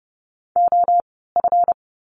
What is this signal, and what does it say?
Morse code, heard as a keyed tone, about 30 wpm, spaced as for 17 wpm, 700 Hz. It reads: OF
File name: course / 30:10wpm / 237 Lesson 38f - Letter q - Words 30wpm